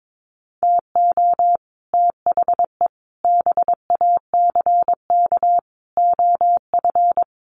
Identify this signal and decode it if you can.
Morse code, heard as a keyed tone, about 22 wpm, 700 Hz. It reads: TO THE BACK OF
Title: Sets of 5 Words 22wpm